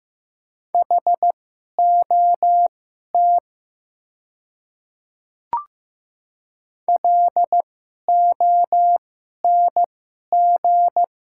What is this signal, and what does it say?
Morse code, heard as a keyed tone, about 15 wpm, spaced as for 11 wpm, 700 Hz. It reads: HOT LONG
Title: 2x Character Spacing - Top 200 Words - 15wpm